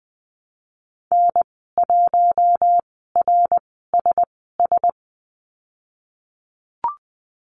Morse code, heard as a keyed tone, about 20 wpm, spaced as for 15 wpm, 700 Hz. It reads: N1RSS
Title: 055 Lesson 10g - Letter r - Call Signs 20wpm